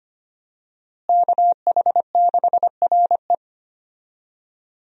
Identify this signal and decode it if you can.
Morse code, heard as a keyed tone, about 25 wpm, 700 Hz. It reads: KH6RE
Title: USA Call Signs - 5 Characters - Repeated 2x - 25wpm